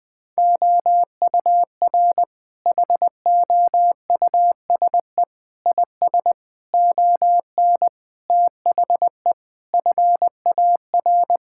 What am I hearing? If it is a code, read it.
Morse code, heard as a keyed tone, about 19 wpm, 700 Hz. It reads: OUR HOUSE IS ON THE FAR
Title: Sentences with 5-Word Phrases - 20wpm